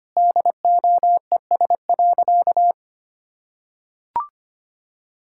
Morse code, heard as a keyed tone, about 25 wpm, 700 Hz. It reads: DOES.
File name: practice / 25wpm / Sentences with 2-Word Phrases - 25wpm